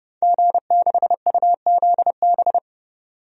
Morse code, heard as a keyed tone, about 30 wpm, 700 Hz. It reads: G6UZB
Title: DXCC-100 Call Signs - 5 Characters - 30wpm